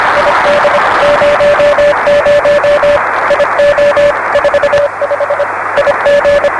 Morse code, heard as a keyed tone, about 25 wpm, 560 Hz, with radio noise. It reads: ID 00 IO 45 IG
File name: RS-7 Beacon 16 Feb 1985 oz6bl